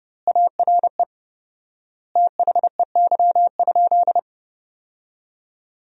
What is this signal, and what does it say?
Morse code, heard as a keyed tone, about 32 wpm, 700 Hz. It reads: ARE THEY?
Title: 4x Word-Spacing - Sentences from Top 200 Words - 30wpm